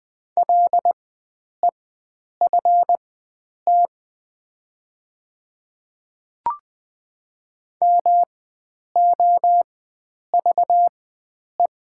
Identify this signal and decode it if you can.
Morse code, heard as a keyed tone, about 20 wpm, spaced as for 9 wpm, 700 Hz. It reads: LEFT MOVE
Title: 4x Character Spacing - Top 500 Words - 20wpm